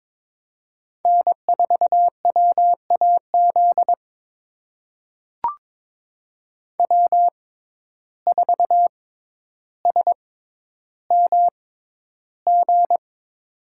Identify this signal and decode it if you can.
Morse code, heard as a keyed tone, about 23 wpm, spaced as for 11 wpm, 700 Hz. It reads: N4WAZ W4SMG
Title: USA Call Signs - 5 Characters - 6x Character Spacing - 22wpm